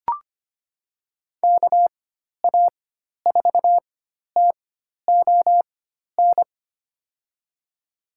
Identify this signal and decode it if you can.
Morse code, heard as a keyed tone, about 25 wpm, spaced as for 12 wpm, 700 Hz. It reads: KA4TON
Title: USA Call Signs - 6 Characters - 4x Character Spacing - 25wpm